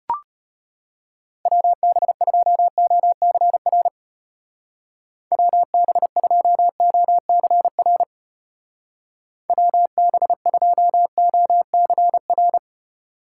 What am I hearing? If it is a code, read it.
Morse code, heard as a keyed tone, about 32 wpm, 700 Hz. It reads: WB2OCR WB2OCR WB2OCR